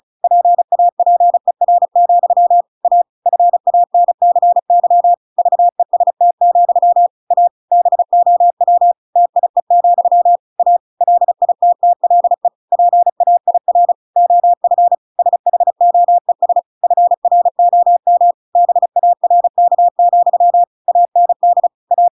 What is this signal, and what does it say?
Morse code, heard as a keyed tone, about 34 wpm, 700 Hz. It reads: PAPER, A FANCY VEST, A BOW TIE, A LITTLE PAIR OF SHOES FROM BARK, AND A
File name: Pinocchio / Pinocchio ch8 35 35 ik1hge